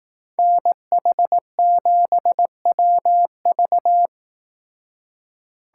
Morse code, heard as a keyed tone, about 18 wpm, 700 Hz. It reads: NH7WV
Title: Speed Racing - USA 2x2 Call Signs 15wpm